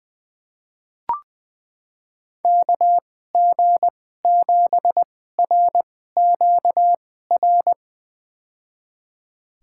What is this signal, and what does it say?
Morse code, heard as a keyed tone, about 20 wpm, spaced as for 14 wpm, 700 Hz. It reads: KG7RQR
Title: USA Call Signs - 6 Characters - 2x Character Spacing - 20wpm